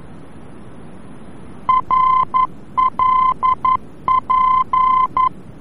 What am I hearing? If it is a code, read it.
Morse code, heard as a keyed tone, about 11 wpm, 1030 Hz. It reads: RLP
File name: france / mx / RLP-VOR-F